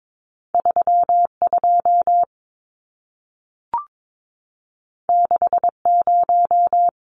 Morse code, heard as a keyed tone, about 22 wpm, 700 Hz. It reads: 32 60
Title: Numbers - 2-Digit 22wpm